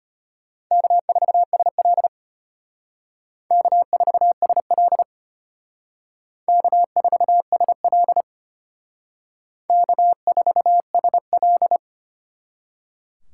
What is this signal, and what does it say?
Morse code, heard as a keyed tone, about 32 wpm, 700 Hz. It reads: K4SL K4SL K4SL K4SL